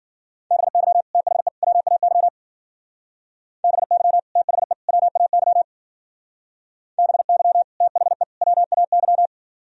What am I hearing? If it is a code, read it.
Morse code, heard as a keyed tone, about 51 wpm, 700 Hz. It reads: BY THE WAY BY THE WAY BY THE WAY